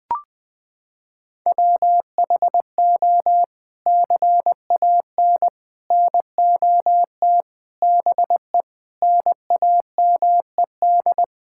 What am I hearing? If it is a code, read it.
Morse code, heard as a keyed tone, about 21 wpm, 700 Hz. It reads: WHO CAN NOT BE NAMED
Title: Sets of 8 Words - 20wpm